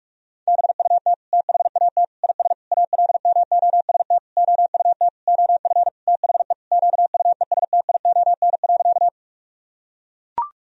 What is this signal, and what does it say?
Morse code, heard as a keyed tone, about 43 wpm, 700 Hz. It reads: BUT THAT IS ALMOST OUT OF THE QUESTION.
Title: Sentences from Top 1000 Words - Review Part 4 - 45wpm